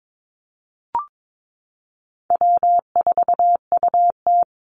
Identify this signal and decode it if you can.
Morse code, heard as a keyed tone, about 22 wpm, 700 Hz. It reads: W4UT
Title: Easy Call Signs Repeated 2x - 22wpm